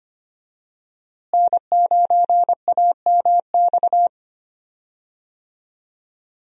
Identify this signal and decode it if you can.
Morse code, heard as a keyed tone, about 25 wpm, 700 Hz. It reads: N9AMX
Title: USA Call Signs - 5 Characters - Repeated 3x - 25wpm